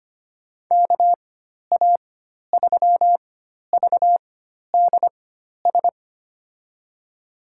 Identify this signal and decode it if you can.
Morse code, heard as a keyed tone, about 25 wpm, spaced as for 12 wpm, 700 Hz. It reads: KA3VDS
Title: USA Call Signs - 6 Characters - 4x Character Spacing - 25wpm